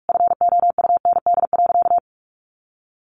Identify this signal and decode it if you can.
Morse code, heard as a keyed tone, about 44 wpm, 700 Hz. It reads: FOUND.
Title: Sentences from Top 1500 Words 45wpm